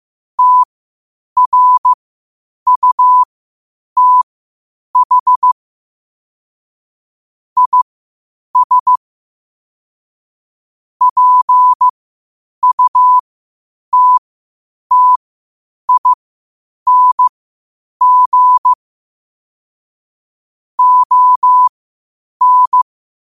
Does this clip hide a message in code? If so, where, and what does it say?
Yes: Morse code from 0.4 to 22.8 s, heard as a keyed tone, about 15 wpm, spaced as for 8 wpm, 1000 Hz. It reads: TRUTH IS PUTTING ON